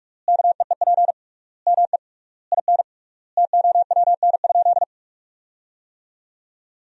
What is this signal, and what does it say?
Morse code, heard as a keyed tone, about 46 wpm, 700 Hz. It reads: KEEP ME IN TOWN?
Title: 3x Word-Spacing - Sentences from Top 500 Words - 45wpm